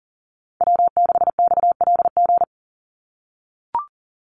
Morse code, heard as a keyed tone, about 40 wpm, 700 Hz. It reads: W6XLG
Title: USA Call Signs 40wpm